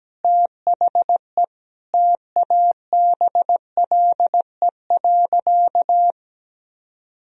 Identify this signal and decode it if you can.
Morse code, heard as a keyed tone, about 17 wpm, 700 Hz. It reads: THE TABLE.
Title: Sentences from Top 1000 Words - Review Part 3 - 17wpm